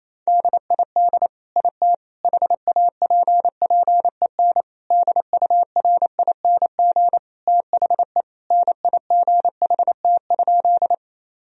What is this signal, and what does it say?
Morse code, heard as a keyed tone, about 27 wpm, 700 Hz. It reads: DID IT HAPPEN DURING THE NIGHT?
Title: Sentences from Top 1000 Words - Review Part 2 - 28wpm